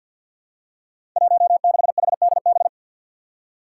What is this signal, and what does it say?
Morse code, heard as a keyed tone, about 46 wpm, 700 Hz. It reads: 16HDB
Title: Five Challenging Letter-Number Combinations 50wpm